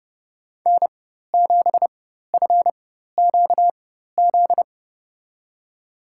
Morse code, heard as a keyed tone, about 30 wpm, spaced as for 14 wpm, 700 Hz. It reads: N7FQZ
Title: USA Call Signs - 4x Character Spacing - 30wpm